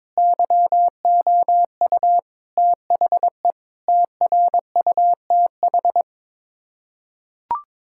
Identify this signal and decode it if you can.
Morse code, heard as a keyed tone, about 22 wpm, 700 Hz. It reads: YOU THE TRUTH
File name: practice / 22wpm / Build A Phrase - 5-Word Phrases - 22wpm